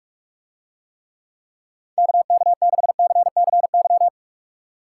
Morse code, heard as a keyed tone, about 47 wpm, 700 Hz. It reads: KK6CCY